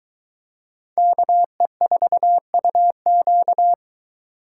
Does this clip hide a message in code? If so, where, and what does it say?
Yes: Morse code from 1.0 to 3.7 s, heard as a keyed tone, about 22 wpm, 700 Hz. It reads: KE4UQ